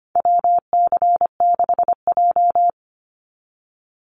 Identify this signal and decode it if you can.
Morse code, heard as a keyed tone, about 25 wpm, 700 Hz. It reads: WC6J
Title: NA SOTA Activator Callsigns 25wpm